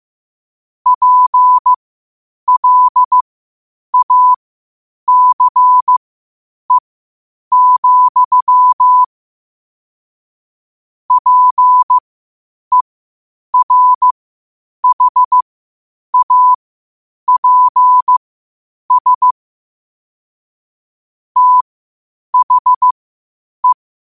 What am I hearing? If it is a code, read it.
Morse code, heard as a keyed tone, about 15 wpm, spaced as for 8 wpm, 1000 Hz. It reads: PLACE, PERHAPS THE
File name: qotdfiles / 8wpm